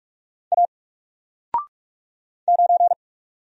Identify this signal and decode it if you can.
Morse code, heard as a keyed tone, about 46 wpm, 700 Hz. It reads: A 9
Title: Single Letter-Number 45wpm